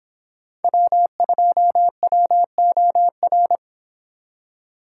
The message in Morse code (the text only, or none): W2WOR